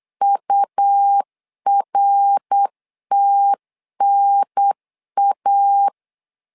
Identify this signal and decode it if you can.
Morse code, heard as a keyed tone, about 8 wpm, 800 Hz. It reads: URTNA
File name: LEZIONI / 3 MWRO / 3 MWR 8wpm 2444